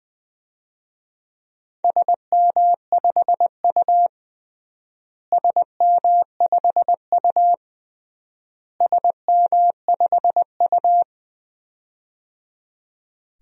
Morse code, heard as a keyed tone, about 20 wpm, 700 Hz. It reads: SM5U SM5U SM5U
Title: DXCC-100 Call Signs - 4 Characters - Repeated 3x - 20wpm